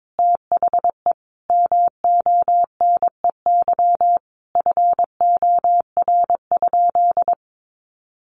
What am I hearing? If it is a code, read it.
Morse code, heard as a keyed tone, about 22 wpm, 700 Hz. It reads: THE MONEY FOR?
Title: Sentences from Top 400 Words 22wpm